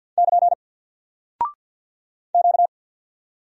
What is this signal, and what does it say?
Morse code, heard as a keyed tone, about 48 wpm, 700 Hz. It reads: <KN> Q
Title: Single Character (Letter:Number:Punctuation:Prosign) 50wpm